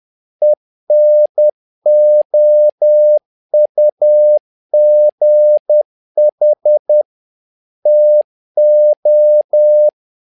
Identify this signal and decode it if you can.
Morse code, heard as a keyed tone, about 10 wpm, 600 Hz. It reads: ENOUGH TO
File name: learning center / tower trouble / tower trouble 10wpm